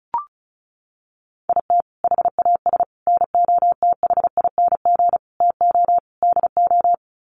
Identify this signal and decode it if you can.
Morse code, heard as a keyed tone, about 34 wpm, 700 Hz. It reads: IT HAS NOTHING TO DO
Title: Sets of 5 Words Encore 35wpm